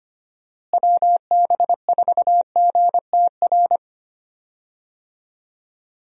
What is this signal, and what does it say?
Morse code, heard as a keyed tone, about 25 wpm, 700 Hz. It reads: WB4GTR
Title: USA Call Signs - 6 Characters - Repeated 3x - 25wpm